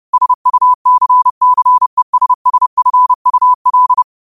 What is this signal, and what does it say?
Morse code, heard as a keyed tone, about 30 wpm, 1000 Hz. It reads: SUCCESSFUL
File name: qotdfiles / 30wpm